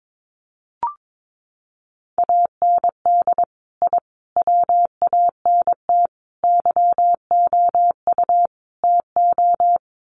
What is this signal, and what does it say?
Morse code, heard as a keyed tone, about 22 wpm, 700 Hz. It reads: AND I WANT YOU TO
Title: Sets of 5 Words Encore 22wpm